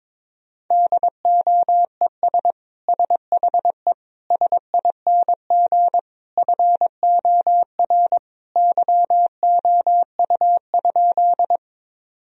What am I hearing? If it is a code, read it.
Morse code, heard as a keyed tone, about 22 wpm, 700 Hz. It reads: DOES SHE SING FOR YOU?
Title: Sentences from Top 1000 Words - Review Part 2 - 22wpm